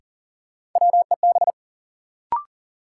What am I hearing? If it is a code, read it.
Morse code, heard as a keyed tone, about 40 wpm, 700 Hz. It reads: WEB
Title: Sentences from Top 1000 Words - Review Part 4 - New Word Drill - 40wpm